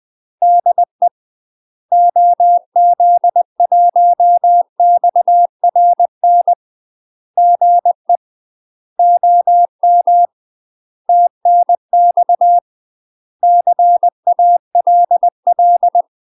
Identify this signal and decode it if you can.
Morse code, heard as a keyed tone, about 20 wpm, 700 Hz. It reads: DE OZ1XRN GE OM TNX CALL
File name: UK MASTER QSOs TRAINING/20WPM